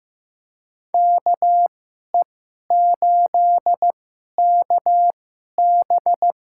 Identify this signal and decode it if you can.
Morse code, heard as a keyed tone, about 15 wpm, spaced as for 11 wpm, 700 Hz. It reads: KE8KB
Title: USA Call Signs - 2x Character Spacing - 15wpm